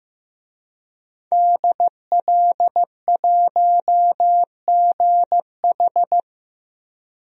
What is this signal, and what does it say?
Morse code, heard as a keyed tone, about 15 wpm, 700 Hz. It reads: DL1GH
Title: DXCC-100 Call Signs - 5 Characters - 15wpm